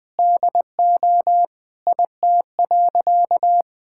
Morse code, heard as a keyed tone, about 20 wpm, 700 Hz. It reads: DO IT.
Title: Sentences with 8-Word Phrases - 20wpm